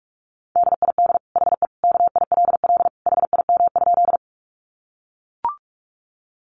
Morse code, heard as a keyed tone, about 44 wpm, 700 Hz. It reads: DID HE KILL HIM?
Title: Sentences from Top 1400 Words 45wpm